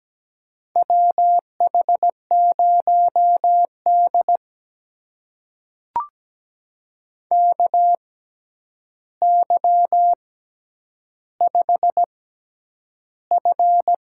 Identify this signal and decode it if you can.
Morse code, heard as a keyed tone, about 17 wpm, spaced as for 9 wpm, 700 Hz. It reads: WH0D KY5F
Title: USA Call Signs - 4 Characters - 6x Character Spacing - 17wpm